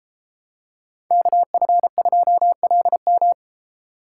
Morse code, heard as a keyed tone, about 32 wpm, 700 Hz. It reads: KF2LM